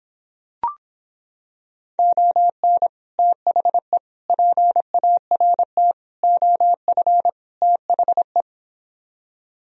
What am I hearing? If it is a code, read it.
Morse code, heard as a keyed tone, about 26 wpm, 700 Hz. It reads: ON THE PART OF THE